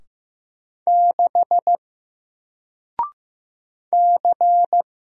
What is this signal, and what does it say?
Morse code, heard as a keyed tone, about 15 wpm, 700 Hz. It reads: 6 C
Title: Single Letter-Number 15wpm